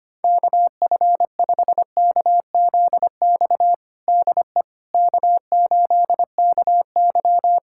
Morse code, heard as a keyed tone, about 25 wpm, 700 Hz. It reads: KF5KZX DE K8KY
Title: USA Callsign DE Callsign 25wpm